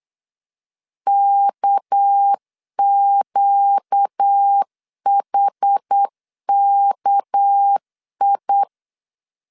Morse code, heard as a keyed tone, about 8 wpm, 800 Hz. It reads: KQHKI
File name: LEZIONI / 6 PKX0 / 6 PK 8wpm 3003